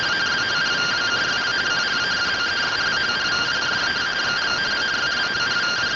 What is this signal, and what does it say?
Two-tone teleprinter signal, about 170 Hz shift.